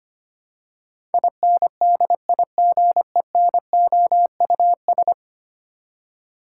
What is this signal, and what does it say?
Morse code, heard as a keyed tone, about 25 wpm, 700 Hz. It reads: INDIGENOUS
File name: 10 Letter Words - First 3 Letters Given - 25wpm